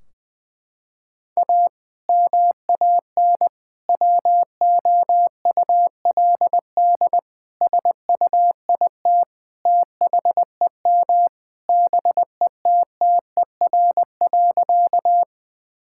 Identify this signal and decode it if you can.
Morse code, heard as a keyed tone, about 20 wpm, 700 Hz. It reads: A MAN WOULD SUIT THEM BETTER.
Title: Sentences from Top 1600 Words 20wpm